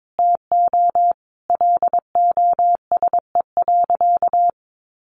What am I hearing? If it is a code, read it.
Morse code, heard as a keyed tone, about 22 wpm, 700 Hz. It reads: TO LOSE.
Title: Sentences from Top 1100 Words 22wpm